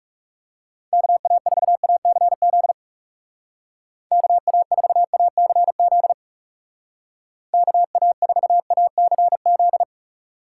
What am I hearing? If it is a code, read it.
Morse code, heard as a keyed tone, about 36 wpm, 700 Hz. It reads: KA4ACZ KA4ACZ KA4ACZ
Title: Speed Racing - USA 2x3 Call Signs 30wpm